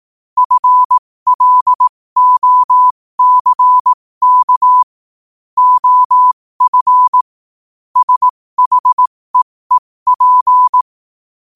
Morse code, heard as a keyed tone, about 18 wpm, spaced as for 15 wpm, 1000 Hz. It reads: FLOCK OF SHEEP
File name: qotdfiles / 15wpm